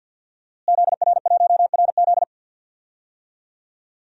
Morse code, heard as a keyed tone, about 52 wpm, 700 Hz. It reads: ZR1RZ
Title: DXCC-100 Call Signs - 5 Characters - Repeated 3x - 50wpm